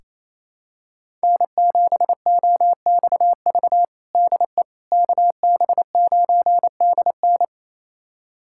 Morse code, heard as a keyed tone, about 28 wpm, 700 Hz. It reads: N7OXV DE KB9DN
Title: USA Callsign DE Callsign - 5 Characters - 28wpm